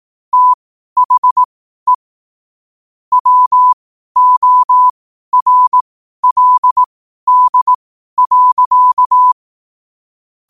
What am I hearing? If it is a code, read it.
Morse code, heard as a keyed tone, about 18 wpm, spaced as for 12 wpm, 1000 Hz. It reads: THE WORLD.
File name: qotdfiles / 12wpm